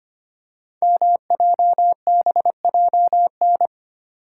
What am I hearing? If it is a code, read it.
Morse code, heard as a keyed tone, about 25 wpm, 700 Hz. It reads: MJBJN
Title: Code Groups - 5 Letters - 25wpm